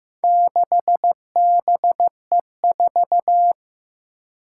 Morse code, heard as a keyed tone, about 15 wpm, 700 Hz. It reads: 6BE4
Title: Four Challenging Letter-Number Combinations 15wpm